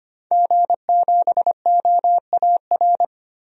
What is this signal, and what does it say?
Morse code, heard as a keyed tone, about 25 wpm, 700 Hz. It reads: G7OAR
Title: DXCC-100 Call Signs - 5 Characters - Repeated 3x - 25wpm